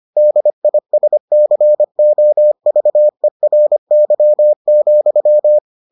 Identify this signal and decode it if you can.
Morse code, heard as a keyed tone, about 25 wpm, 600 Hz. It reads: DISCOVERY,